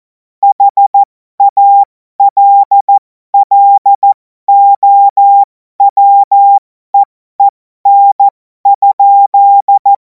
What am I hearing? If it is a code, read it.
Morse code, heard as a keyed tone, about 14 wpm, spaced as for 12 wpm, 800 Hz. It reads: HALLOWEEN?
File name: Morsekurs / Witze